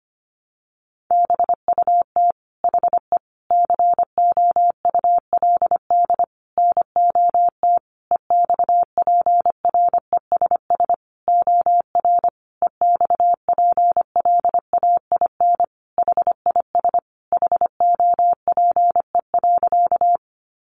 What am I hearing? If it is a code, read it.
Morse code, heard as a keyed tone, about 25 wpm, 700 Hz. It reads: BUT HE COULD NOT EXPRESS OR EXPLAIN HIS HOPE.
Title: Sentences from Top 1100 Words 25wpm